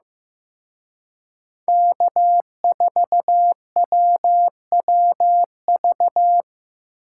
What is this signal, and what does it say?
Morse code, heard as a keyed tone, about 18 wpm, 700 Hz. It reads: K4WWV
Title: Speed Racing - USA 1x3 Call Signs 15wpm